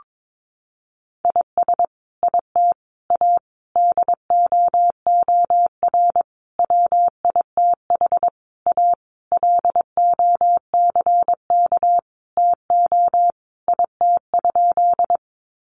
Morse code, heard as a keyed tone, about 22 wpm, 700 Hz. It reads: IS IT A DOOR WITH A LOCK TO IT?
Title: Sentences from Top 1300 Words 22wpm